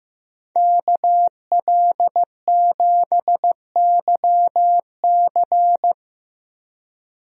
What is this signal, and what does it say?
Morse code, heard as a keyed tone, about 15 wpm, 700 Hz. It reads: KL7YC
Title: DXCC-100 Call Signs - 5 Characters - 15wpm